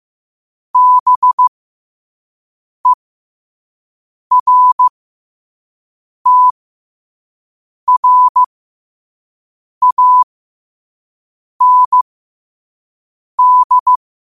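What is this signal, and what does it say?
Morse code, heard as a keyed tone, about 15 wpm, spaced as for 5 wpm, 1000 Hz. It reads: BERTRAND
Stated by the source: Quote-of-the-day Morse code practice at 5 wpm.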